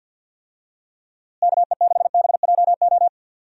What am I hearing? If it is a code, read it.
Morse code, heard as a keyed tone, about 51 wpm, 700 Hz. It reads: KE6BJO
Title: USA Call Signs - 6 Characters - Repeated 3x - 50wpm